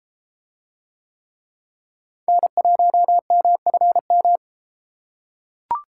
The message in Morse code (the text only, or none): N1MFM